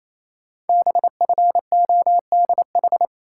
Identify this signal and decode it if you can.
Morse code, heard as a keyed tone, about 29 wpm, 700 Hz. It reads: BFODH
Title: Code Groups - 5 Letters - 28wpm